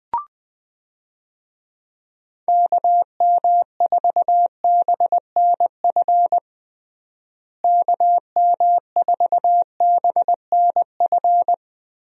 Morse code, heard as a keyed tone, about 20 wpm, 700 Hz. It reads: KM4BNF KM4BNF
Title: USA Call Signs - 6 Characters - Repeated 3x - 20wpm